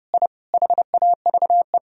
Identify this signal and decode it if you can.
Morse code, heard as a keyed tone, about 30 wpm, 700 Hz. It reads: I HAVE
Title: Sentences from Top 1000 Words - Review Part 2 - 30wpm